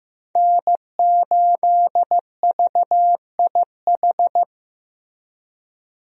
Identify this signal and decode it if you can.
Morse code, heard as a keyed tone, about 15 wpm, 700 Hz. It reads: N8VIH
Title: USA Call Signs - 5 Characters - Repeated 2x - 15wpm